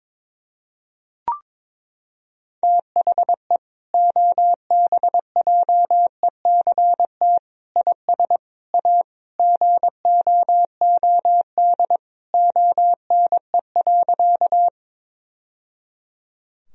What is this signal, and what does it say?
Morse code, heard as a keyed tone, about 22 wpm, 700 Hz. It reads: THE OBJECT IS A GOOD ONE.
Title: Sentences from Top 1000 Words - Review Part 3 - 22wpm